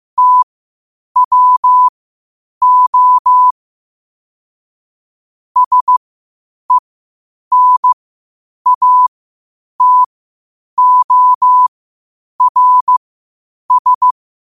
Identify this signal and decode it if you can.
Morse code, heard as a keyed tone, about 15 wpm, spaced as for 8 wpm, 1000 Hz. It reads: TWO SENATORS
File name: qotdfiles / 8wpm